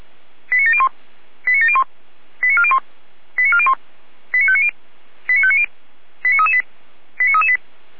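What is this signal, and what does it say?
Five-tone selective call (ZVEI), heard as quick runs of five tones, tones of about 70 ms. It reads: 89701 89701 89401 89401 89590 89590 89338 89338
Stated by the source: ZVEI five-tone selective call received by SDR.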